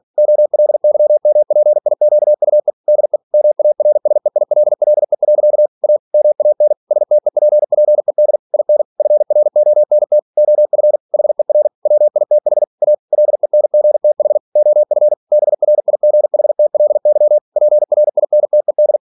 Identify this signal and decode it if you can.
Morse code, heard as a keyed tone, about 46 wpm, 600 Hz. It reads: OLYMPIQUE DE MARSEILLE. A MAN STEPPED IN FRONT OF HER WITH A LENGTH OF BRIGHTLY PRINTED
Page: Never 47wpm